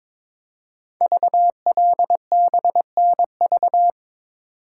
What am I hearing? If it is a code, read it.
Morse code, heard as a keyed tone, about 22 wpm, 700 Hz. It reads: VLBNV
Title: Code Groups - 5 Letter:Numbers - 22wpm